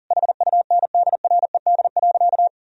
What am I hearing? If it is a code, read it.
Morse code, heard as a keyed tone, about 40 wpm, 700 Hz. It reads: HUNDRED.
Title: Sentences from Top 1000 Words - Review Part 2 - 40wpm